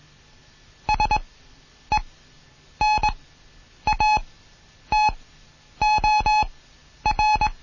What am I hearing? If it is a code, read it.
Morse code, heard as a keyed tone, about 21 wpm, spaced as for 9 wpm, 850 Hz. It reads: SENATOR
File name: Morsen / Kurs A 20